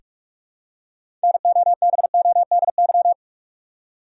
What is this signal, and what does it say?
Morse code, heard as a keyed tone, about 46 wpm, 700 Hz. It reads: NOBODY
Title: Six Letter Words - 45wpm